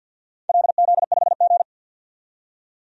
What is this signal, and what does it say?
Morse code, heard as a keyed tone, about 53 wpm, 700 Hz. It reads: LZ5G
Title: DXCC-100 Call Signs - 4 Characters - 50wpm